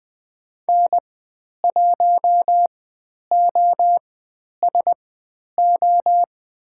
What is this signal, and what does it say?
Morse code, heard as a keyed tone, about 20 wpm, spaced as for 10 wpm, 700 Hz. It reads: N1OSO
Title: 102 Lesson 17g - Letter u - Call Signs 20wpm